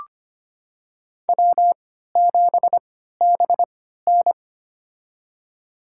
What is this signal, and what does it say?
Morse code, heard as a keyed tone, about 25 wpm, spaced as for 14 wpm, 700 Hz. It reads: W7BN
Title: USA Call Signs - 3x Character Spacing - 25wpm